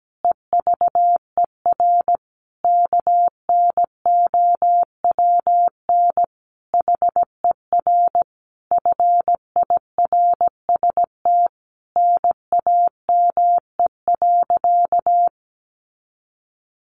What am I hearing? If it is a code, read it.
Morse code, heard as a keyed tone, about 17 wpm, 700 Hz. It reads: EVER KNOWN HER FIRST NAME.
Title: Sentences from Top 1200 Words 17wpm